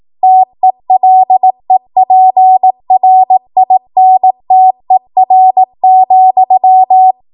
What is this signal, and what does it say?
Morse code, heard as a keyed tone, about 18 wpm, 750 Hz. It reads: TELEPRINTER,
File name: Morse / 18WPM